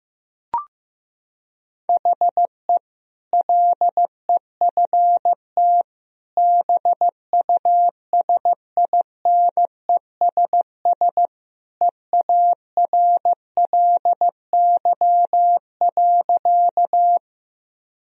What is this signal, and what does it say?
Morse code, heard as a keyed tone, about 15 wpm, 700 Hz. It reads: HE LEFT BUSINESS EARLY.
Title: Sentences from Top 1000 Words - Review Part 2 - 15wpm